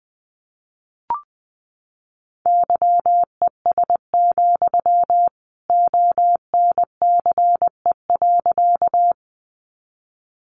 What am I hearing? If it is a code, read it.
Morse code, heard as a keyed tone, about 20 wpm, 700 Hz. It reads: YES, ONCE.
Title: Sentences from Top 500 Words 20wpm